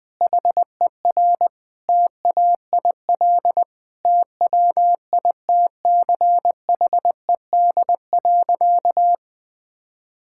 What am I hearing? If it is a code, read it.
Morse code, heard as a keyed tone, about 20 wpm, 700 Hz. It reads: HER TAIL TWITCHED.
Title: Sentences with 2-Word Phrases - 20wpm